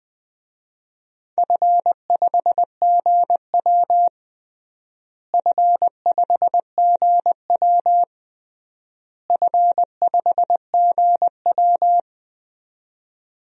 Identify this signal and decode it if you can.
Morse code, heard as a keyed tone, about 20 wpm, 700 Hz. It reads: F5GW F5GW F5GW
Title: DXCC-100 Call Signs - 4 Characters - Repeated 3x - 20wpm